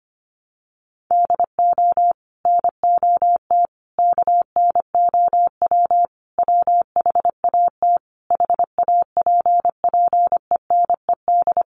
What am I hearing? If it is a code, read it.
Morse code, heard as a keyed tone, about 25 wpm, 700 Hz. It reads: DO NOT KNOW WHAT HAPPENED
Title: Sets of 5 Words Encore2 25wpm